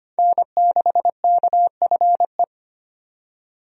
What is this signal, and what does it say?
Morse code, heard as a keyed tone, about 25 wpm, 700 Hz. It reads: N6KFE
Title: USA Call Signs - 5 Characters - 25wpm